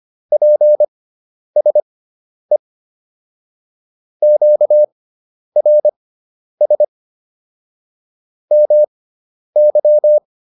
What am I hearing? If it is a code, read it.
Morse code, heard as a keyed tone, about 25 wpm, spaced as for 10 wpm, 600 Hz. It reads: PSE QRS MY